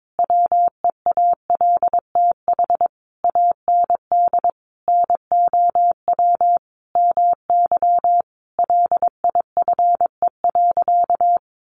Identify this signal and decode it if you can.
Morse code, heard as a keyed tone, about 22 wpm, 700 Hz. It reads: WEALTH AND NOW MY LIFE.
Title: Sentences from Top 1200 Words 22wpm